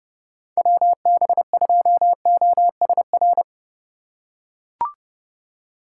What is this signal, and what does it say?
Morse code, heard as a keyed tone, about 30 wpm, 700 Hz. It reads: WB2OSR